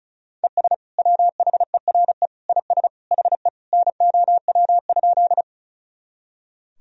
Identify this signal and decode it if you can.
Morse code, heard as a keyed tone, about 34 wpm, 700 Hz. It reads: ES WHERE IS HE NOW?
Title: Sentences from Top 200 Words and Abbreviations 35wpm